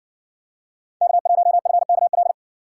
Morse code, heard as a keyed tone, about 60 wpm, 700 Hz. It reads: K1FKL